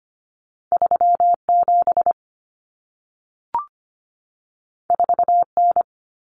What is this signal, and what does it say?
Morse code, heard as a keyed tone, about 25 wpm, 700 Hz. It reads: 37 4N
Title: Common Cut Numbers - 2-Digit 25wpm